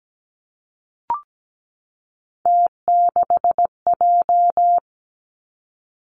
Morse code, heard as a keyed tone, about 17 wpm, 700 Hz. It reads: T6J
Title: Three Challenging Letter-Number Combinations 17wpm